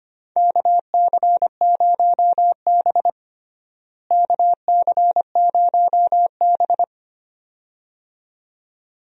Morse code, heard as a keyed tone, about 25 wpm, 700 Hz. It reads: KC0B KC0B
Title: USA Call Signs - 4 Characters - Repeated 2x - 25wpm